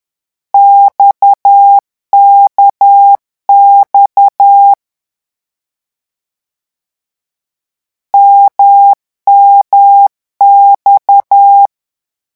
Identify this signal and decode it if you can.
Morse code, heard as a keyed tone, about 10 wpm, 800 Hz. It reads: XKX MMX
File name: morse mp3 / lecon1-vitesse 10